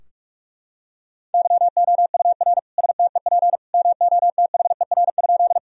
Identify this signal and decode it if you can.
Morse code, heard as a keyed tone, about 45 wpm, 700 Hz. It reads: YOUR STEP MOTHER?
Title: Sentences from Top 1000 Words - Review Part 2 - 45wpm